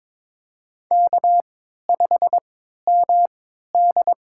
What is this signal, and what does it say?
Morse code, heard as a keyed tone, about 23 wpm, spaced as for 13 wpm, 700 Hz. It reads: K5MD
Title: USA Call Signs - 3x Character Spacing - 22wpm